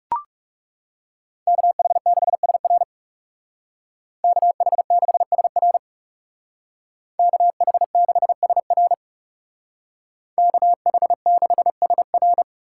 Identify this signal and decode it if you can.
Morse code, heard as a keyed tone, about 37 wpm, 700 Hz. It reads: KH6SR KH6SR KH6SR KH6SR
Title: Speed Racing - USA 2x2 Call Signs 30wpm